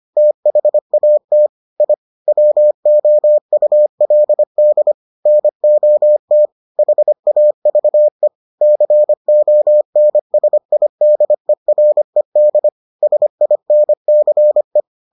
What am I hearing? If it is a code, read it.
Morse code, heard as a keyed tone, about 25 wpm, 600 Hz. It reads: THAT I WOULD NOT HAVE CONSIDERED SINCE